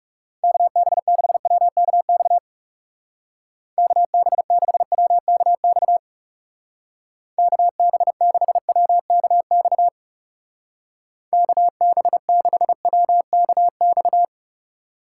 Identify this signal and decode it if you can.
Morse code, heard as a keyed tone, about 35 wpm, 700 Hz. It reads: KB6WKX KB6WKX KB6WKX KB6WKX
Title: Speed Racing - USA 2x3 Call Signs 30wpm